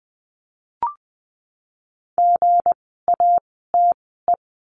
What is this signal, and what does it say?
Morse code, heard as a keyed tone, about 20 wpm, spaced as for 14 wpm, 700 Hz. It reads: GATE
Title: 160 Lesson 26f - Letter g - Words 20wpm